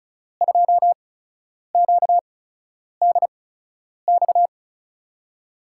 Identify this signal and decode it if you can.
Morse code, heard as a keyed tone, about 36 wpm, spaced as for 10 wpm, 700 Hz. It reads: 2QDX
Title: USA Call Signs - 6 Characters - 8x Character Spacing - 35wpm